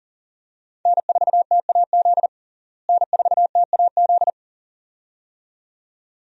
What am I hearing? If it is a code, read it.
Morse code, heard as a keyed tone, about 40 wpm, 700 Hz. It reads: N4TAZ N4TAZ
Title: USA Call Signs - 5 Characters - Repeated 2x - 40wpm